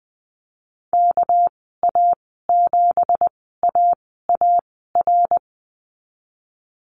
Morse code, heard as a keyed tone, about 20 wpm, spaced as for 15 wpm, 700 Hz. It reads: KA7AAR
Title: 207 Lesson 33g - Letter j - Call Signs 20wpm